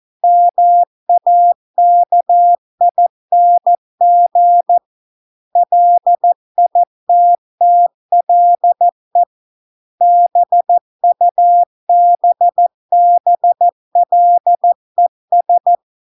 Morse code, heard as a keyed tone, about 14 wpm, 700 Hz. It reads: MAKING LITTLE BUBBLES